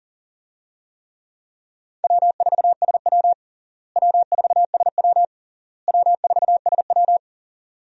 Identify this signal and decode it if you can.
Morse code, heard as a keyed tone, about 40 wpm, 700 Hz. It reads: W4SW W4SW W4SW
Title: USA Call Signs - 4 Characters - Repeated 3x - 40wpm